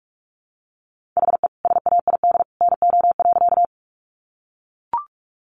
Morse code, heard as a keyed tone, about 44 wpm, 700 Hz. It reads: HE SAID NO.
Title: Sentences from Top 100 Words 45wpm